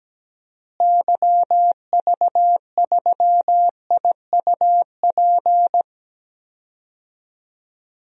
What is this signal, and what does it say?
Morse code, heard as a keyed tone, about 17 wpm, 700 Hz. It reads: YV3IUP
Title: DXCC-100 Call Signs - 6 Characters - Repeated 3x - 17wpm